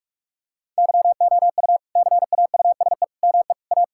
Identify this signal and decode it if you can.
Morse code, heard as a keyed tone, about 42 wpm, 700 Hz. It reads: YOU CAUSE ME A
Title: Sentences from Top 1600 Words 45wpm